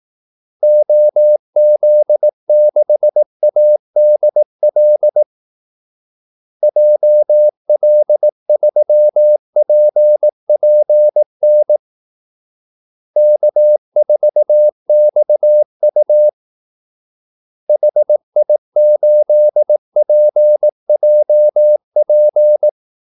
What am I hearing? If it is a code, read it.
Morse code, heard as a keyed tone, about 18 wpm, 600 Hz. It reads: OZ6ADL JL3PPN K4XU HI8PJP
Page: Kaldesignaler 18 wpm